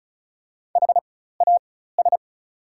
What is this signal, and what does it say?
Morse code, heard as a keyed tone, about 36 wpm, spaced as for 16 wpm, 700 Hz. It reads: HAS
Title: 4x Character Spacing - Top 200 Words - 35wpm